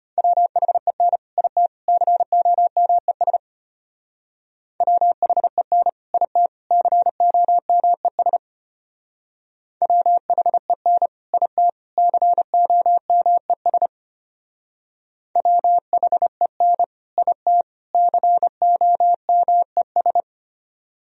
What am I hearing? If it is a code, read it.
Morse code, heard as a keyed tone, about 33 wpm, 700 Hz. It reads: WHEN IT COMES WHEN IT COMES WHEN IT COMES WHEN IT COMES